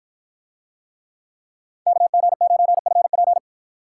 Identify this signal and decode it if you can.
Morse code, heard as a keyed tone, about 53 wpm, 700 Hz. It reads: KG9FP